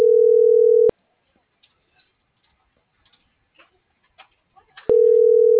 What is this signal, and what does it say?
Ringback tone (North America), 440 + 480 Hz.